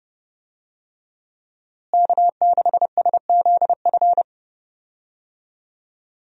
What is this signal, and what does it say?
Morse code, heard as a keyed tone, about 31 wpm, 700 Hz. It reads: K6SZF
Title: USA Call Signs - 5 Characters - Repeated 3x - 30wpm